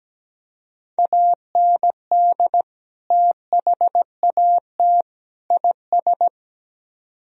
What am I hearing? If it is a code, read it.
Morse code, heard as a keyed tone, about 18 wpm, 700 Hz. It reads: AND THAT IS
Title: Speed-Racing - Sets of 3 Words 17wpm